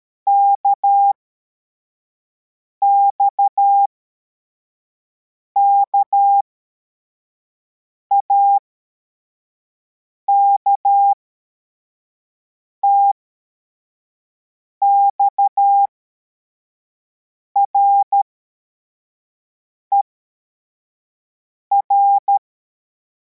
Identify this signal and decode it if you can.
Morse code, heard as a keyed tone, about 13 wpm, spaced as for 4 wpm, 800 Hz. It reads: KXKAK TXRER
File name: formation / cw / Text25